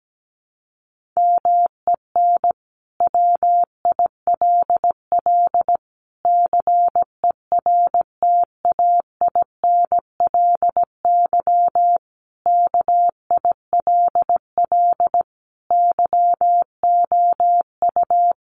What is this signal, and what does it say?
Morse code, heard as a keyed tone, about 17 wpm, 700 Hz. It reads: MEN WILL CERTAINLY KILL YOU
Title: Sentences from Top 1500 Words 17wpm